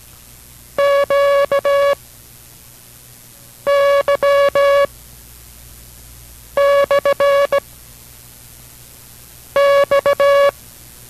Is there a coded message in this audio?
Yes: Morse code from 0.8 to 10.5 s, heard as a keyed tone, about 17 wpm, 560 Hz. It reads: Q Y / X